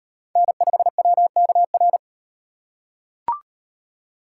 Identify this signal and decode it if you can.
Morse code, heard as a keyed tone, about 38 wpm, 700 Hz. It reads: N5WKR